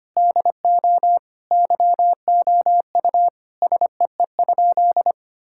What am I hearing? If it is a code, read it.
Morse code, heard as a keyed tone, about 25 wpm, 700 Hz. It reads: DO YOU SEE?
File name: practice / 25wpm / Sentences with 3-Word Phrases - 25wpm